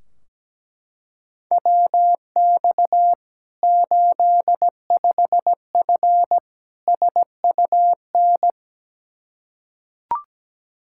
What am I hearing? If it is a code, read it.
Morse code, heard as a keyed tone, about 17 wpm, 700 Hz. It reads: WX 85F SUN
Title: Practice QSO - Exchange 2 - Info - 17wpm